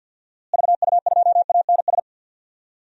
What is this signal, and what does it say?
Morse code, heard as a keyed tone, about 53 wpm, 700 Hz. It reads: VU2ANS
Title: DXCC-100 Call Signs - 6 Characters - Repeated 3x - 50wpm